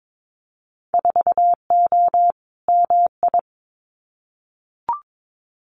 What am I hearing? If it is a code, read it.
Morse code, heard as a keyed tone, about 22 wpm, 700 Hz. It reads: 4O MI
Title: Winter Field Day Exchanges Encore 22wpm